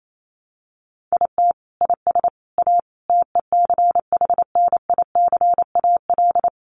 Morse code, heard as a keyed tone, about 28 wpm, 700 Hz. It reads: IT IS A TECHNICAL
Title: Sentences from Top 1100 Words 28wpm